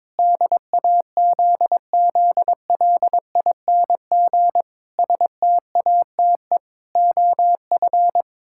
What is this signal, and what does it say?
Morse code, heard as a keyed tone, about 22 wpm, 700 Hz. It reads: DAZZLING STATE OF
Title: Pangram Challenge v2 - 22wpm